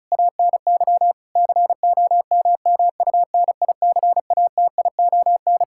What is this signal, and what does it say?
Morse code, heard as a keyed tone, about 36 wpm, 700 Hz. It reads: ANY COMMUNICATION
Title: Sentences from Top 1000 Words - Review Part 6 - 35wpm